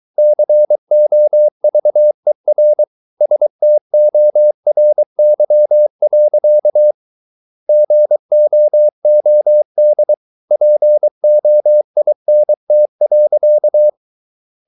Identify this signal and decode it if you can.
Morse code, heard as a keyed tone, about 23 wpm, 600 Hz. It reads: COVER STORY. GOOD POINT.